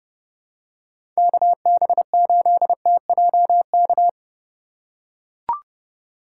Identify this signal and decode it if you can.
Morse code, heard as a keyed tone, about 31 wpm, 700 Hz. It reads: KB8TJK
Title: USA Call Signs - 2x Character Spacing - 30wpm